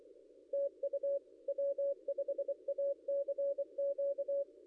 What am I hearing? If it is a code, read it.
Morse code, heard as a keyed tone, about 24 wpm, 560 Hz. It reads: TU W5ACQ
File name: contests / ss-cw-k5tr-op-n5rz -0800